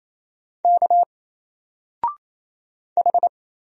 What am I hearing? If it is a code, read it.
Morse code, heard as a keyed tone, about 27 wpm, 700 Hz. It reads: K H
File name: Single-Letter-Number-28wpm